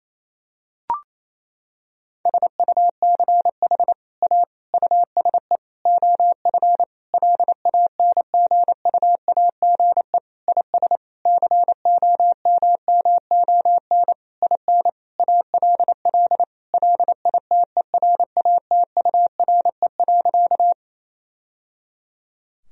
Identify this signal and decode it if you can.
Morse code, heard as a keyed tone, about 27 wpm, 700 Hz. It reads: SUCH A USE OF LANGUAGE IS COMMON IN ALL LITERATURE.
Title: Sentences from Top 1000 Words - Review Part 3 - 28wpm